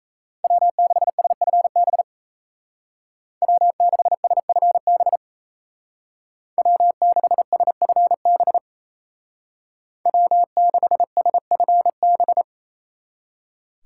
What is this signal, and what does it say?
Morse code, heard as a keyed tone, about 36 wpm, 700 Hz. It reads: W6SFB W6SFB W6SFB W6SFB